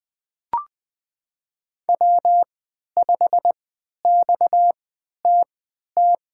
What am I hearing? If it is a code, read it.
Morse code, heard as a keyed tone, about 20 wpm, spaced as for 11 wpm, 700 Hz. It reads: W5XTT
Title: USA Call Signs - 5 Characters - 3x Character Spacing - 20wpm